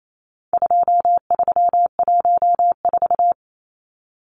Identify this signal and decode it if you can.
Morse code, heard as a keyed tone, about 28 wpm, 700 Hz. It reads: 2314
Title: Numbers - 4-Digit 28wpm